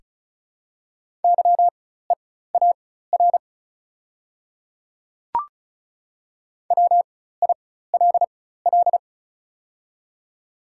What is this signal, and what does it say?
Morse code, heard as a keyed tone, about 36 wpm, spaced as for 16 wpm, 690 Hz. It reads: YEAR WILL
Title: 4x Character Spacing - Top 200 Words - 35wpm